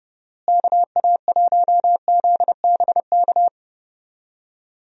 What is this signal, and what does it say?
Morse code, heard as a keyed tone, about 30 wpm, 700 Hz. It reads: KA1ZBK
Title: USA Call Signs - 6 Characters - Repeated 3x - 30wpm